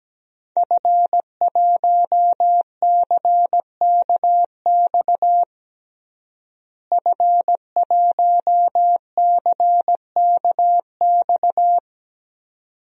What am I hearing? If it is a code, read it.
Morse code, heard as a keyed tone, about 17 wpm, 700 Hz. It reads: F1CKX F1CKX
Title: DXCC-100 Call Signs - 5 Characters - Repeated 2x - 17wpm